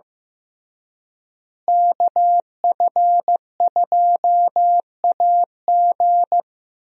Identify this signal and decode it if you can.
Morse code, heard as a keyed tone, about 16 wpm, 700 Hz. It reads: KF2AG
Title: USA Call Signs - 5 Characters - Repeated 3x - 15wpm